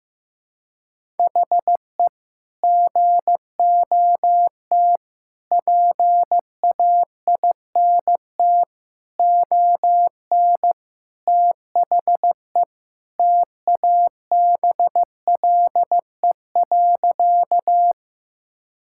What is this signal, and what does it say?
Morse code, heard as a keyed tone, about 16 wpm, 700 Hz. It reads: HE GOT PAINT ON THE TABLE.
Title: Sentences from Top 1000 Words - Review Part 6 - 15wpm